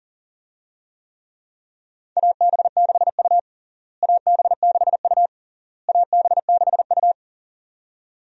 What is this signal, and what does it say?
Morse code, heard as a keyed tone, about 40 wpm, 700 Hz. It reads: AB6U AB6U AB6U
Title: USA Call Signs - 4 Characters - Repeated 3x - 40wpm